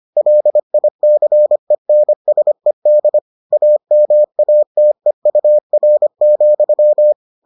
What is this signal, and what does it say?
Morse code, heard as a keyed tone, about 25 wpm, 600 Hz. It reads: LICENSED AMATEUR,